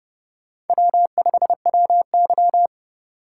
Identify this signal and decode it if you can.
Morse code, heard as a keyed tone, about 28 wpm, 700 Hz. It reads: W5WY